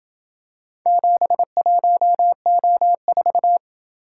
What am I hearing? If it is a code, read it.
Morse code, heard as a keyed tone, about 27 wpm, 700 Hz. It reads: 71O4